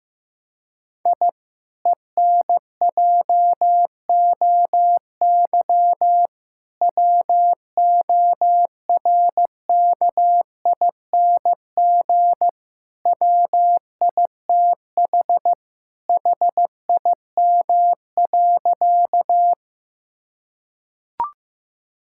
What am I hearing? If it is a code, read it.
Morse code, heard as a keyed tone, about 15 wpm, 700 Hz. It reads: I ENJOY WORKING WITH HIM.
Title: Sentences from Top 1000 Words - Review Part 5 - 15wpm